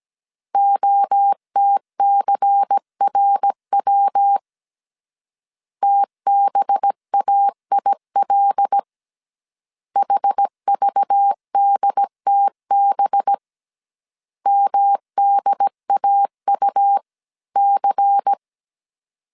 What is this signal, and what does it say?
Morse code, heard as a keyed tone, about 17 wpm, 800 Hz. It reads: OTCRW TBAIL HVDTB MDAU C
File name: 16wpm